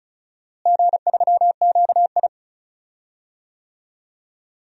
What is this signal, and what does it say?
Morse code, heard as a keyed tone, about 34 wpm, 700 Hz. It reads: G3QI
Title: DXCC-100 Call Signs - 4 Characters - Repeated 3x - 35wpm